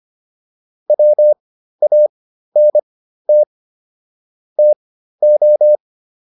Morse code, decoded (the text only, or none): WANT TO